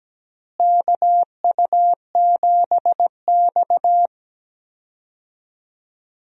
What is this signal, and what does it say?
Morse code, heard as a keyed tone, about 17 wpm, 700 Hz. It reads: KU7X
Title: USA Call Signs - 4 Characters - Repeated 2x - 17wpm